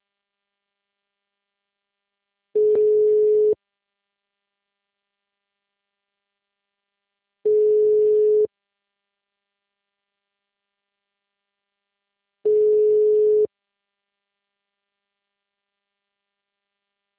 Ringback tone (Europe, 425 Hz).